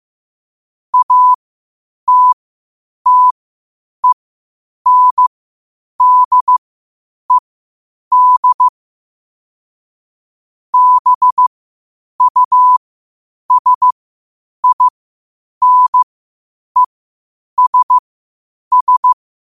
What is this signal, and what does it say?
Morse code, heard as a keyed tone, about 15 wpm, spaced as for 8 wpm, 1000 Hz. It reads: ATTENDED BUSINESS
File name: qotdfiles / 8wpm